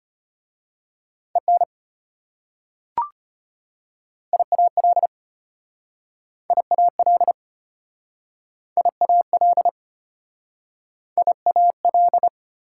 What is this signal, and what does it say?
Morse code, heard as a keyed tone, about 33 wpm, 700 Hz. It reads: EN IAL IAL IAL IAL